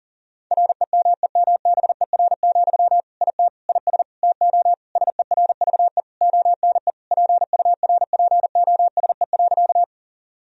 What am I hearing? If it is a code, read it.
Morse code, heard as a keyed tone, about 40 wpm, 700 Hz. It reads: REMEMBER, IT IS TO SERVE ONE PURPOSE.
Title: Sentences from Top 1000 Words - Review Part 2 - 40wpm